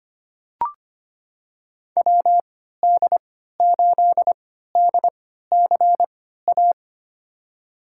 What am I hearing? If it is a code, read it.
Morse code, heard as a keyed tone, about 25 wpm, spaced as for 14 wpm, 700 Hz. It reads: WD8DCA
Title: USA Call Signs - 6 Characters - 3x Character Spacing - 25wpm